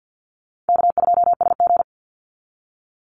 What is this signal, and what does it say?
Morse code, heard as a keyed tone, about 50 wpm, 700 Hz. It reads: K2SG
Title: Call Signs Repeated 3x - 50wpm